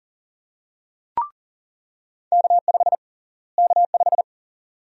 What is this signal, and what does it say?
Morse code, heard as a keyed tone, about 40 wpm, 700 Hz. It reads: K5 K5
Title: USA Call Sign Prefixes - Repeated 2x - 40wpm